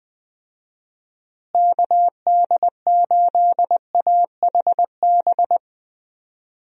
Morse code, heard as a keyed tone, about 21 wpm, 700 Hz. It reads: KD8AHB
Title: USA Call Signs - 6 Characters - Repeated 2x - 20wpm